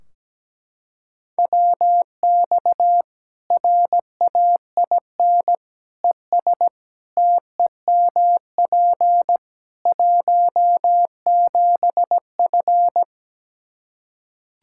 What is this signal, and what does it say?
Morse code, heard as a keyed tone, about 17 wpm, 700 Hz. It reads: WX RAIN ES TEMP 17F
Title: Practice QSO - Exchange 2 - Info - 17wpm